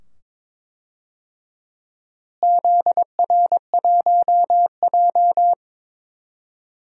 Morse code, heard as a keyed tone, about 21 wpm, 700 Hz. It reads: ZR1J
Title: DXCC-100 Call Signs - 4 Characters - Repeated 2x - 22wpm